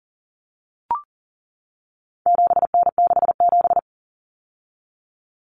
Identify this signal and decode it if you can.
Morse code, heard as a keyed tone, about 40 wpm, 700 Hz. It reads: 7N67
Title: Common Cut Numbers - 4-Digit 40wpm